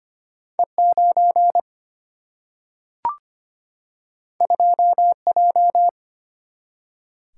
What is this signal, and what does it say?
Morse code, heard as a keyed tone, about 25 wpm, 700 Hz. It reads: E 9 2 J
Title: Single-Letter Single-Number 25wpm